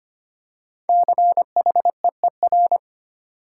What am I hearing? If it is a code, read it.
Morse code, heard as a keyed tone, about 25 wpm, 700 Hz. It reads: CHEER
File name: 5 Letter Words - First 3 Letters Given - 25wpm